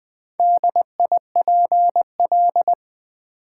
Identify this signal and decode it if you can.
Morse code, heard as a keyed tone, about 20 wpm, 700 Hz. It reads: DIPL
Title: Build A Word - 2 Letters At A Time - 8-Letter Words - 20wpm